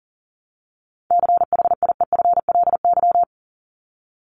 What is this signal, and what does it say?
Morse code, heard as a keyed tone, about 40 wpm, 700 Hz. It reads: CHIEFLY
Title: LY Suffix - Top 100 Words with Base Words 40wpm